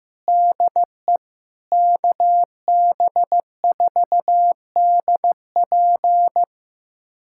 Morse code, heard as a keyed tone, about 15 wpm, 700 Hz. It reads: DE KB4DP
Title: USA Callsign DE Callsign - 5 Characters - 15wpm